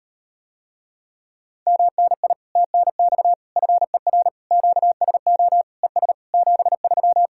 Morse code, heard as a keyed tone, about 38 wpm, 700 Hz. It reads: MNI TNX FER QSO ES 73